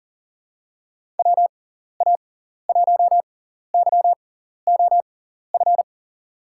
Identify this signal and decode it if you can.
Morse code, heard as a keyed tone, about 39 wpm, spaced as for 14 wpm, 700 Hz. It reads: WA1YOF
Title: USA Call Signs - 6 Characters - 6x Character Spacing - 40wpm